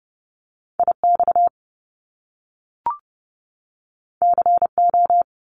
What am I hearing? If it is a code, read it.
Morse code, heard as a keyed tone, about 30 wpm, 700 Hz. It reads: I X C O
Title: Top Two Letter Combinations in Words 30wpm